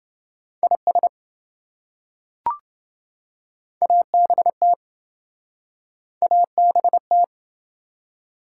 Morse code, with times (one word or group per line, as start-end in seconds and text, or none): 0.6-1.1 IS
3.8-4.7 ABT
6.2-7.2 ABT